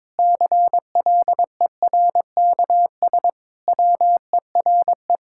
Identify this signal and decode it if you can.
Morse code, heard as a keyed tone, about 22 wpm, 700 Hz. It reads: CLERKS WERE
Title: Pangram Challenge v2 - 22wpm